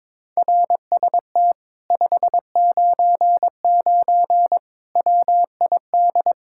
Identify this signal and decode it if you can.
Morse code, heard as a keyed tone, about 22 wpm, 700 Hz. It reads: RST 599 WID
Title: Speed-Racing Common QSO Phrases - 22wpm